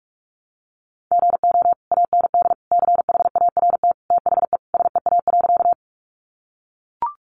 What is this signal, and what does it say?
Morse code, heard as a keyed tone, about 44 wpm, 700 Hz. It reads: GO AND CHART THE SEA.
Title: Sentences from Top 1300 Words 45wpm